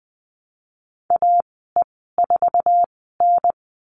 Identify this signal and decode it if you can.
Morse code, heard as a keyed tone, about 20 wpm, spaced as for 15 wpm, 700 Hz. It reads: AE4N
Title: 048 Lesson 9g - Number 4 - Call Signs 20wpm